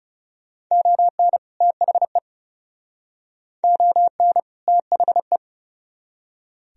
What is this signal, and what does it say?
Morse code, heard as a keyed tone, about 39 wpm, 700 Hz. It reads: ON THE ON THE